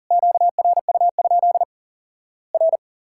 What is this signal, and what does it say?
Morse code, heard as a keyed tone, about 40 wpm, 700 Hz. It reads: QRU? R
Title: Worked All Europe DX Contest - QTC Exchange - 40wpm